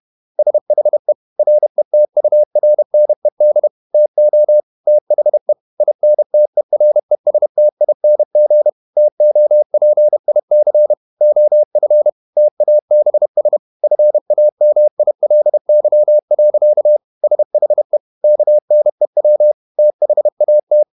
Morse code, heard as a keyed tone, about 31 wpm, 600 Hz. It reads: SHE RETURNED TO THE INTERESTING TOPIC OF TABS FAMILY. SHE KNEW THAT
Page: Never 31wpm | CW med Gnister